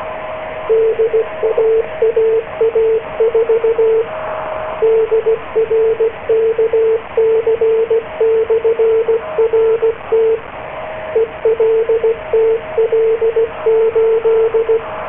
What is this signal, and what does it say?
Morse code, heard as a keyed tone, about 16 wpm, 450 Hz, with radio noise. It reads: DAAA4 DRKC/RT ELTL8